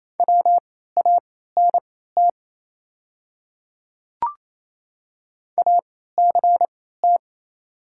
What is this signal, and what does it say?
Morse code, heard as a keyed tone, about 27 wpm, spaced as for 16 wpm, 700 Hz. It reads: WANT ACT
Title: 3x Character Spacing - Top 200 Words - 28wpm